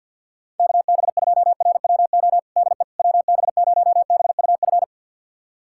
Morse code, heard as a keyed tone, about 46 wpm, 700 Hz. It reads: KB2RWO DE WB0BUF
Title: USA Callsign DE Callsign - 6 Characters - 50wpm